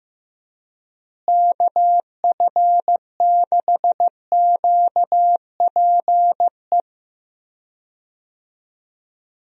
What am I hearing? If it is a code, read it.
Morse code, heard as a keyed tone, about 15 wpm, 700 Hz. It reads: KF6QPE
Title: USA Call Signs - 6 Characters - Repeated 3x - 15wpm